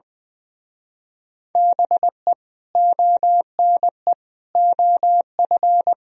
Morse code, heard as a keyed tone, about 20 wpm, 700 Hz. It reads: BE ONE OF
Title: Speed-Racing - Sets of 4 Words 20wpm